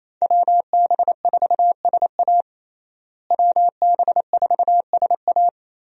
Morse code, heard as a keyed tone, about 28 wpm, 700 Hz. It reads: WB4SA WB4SA
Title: USA Call Signs - 5 Characters - Repeated 2x - 28wpm